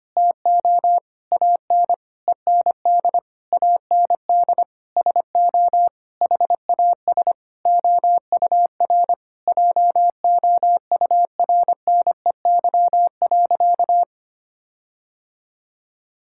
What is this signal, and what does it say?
Morse code, heard as a keyed tone, about 25 wpm, 700 Hz. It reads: TO AN END AND SO HAS OUR JOURNEY.
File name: practice / 25wpm / Sentences with 9-Word Phrases - 25wpm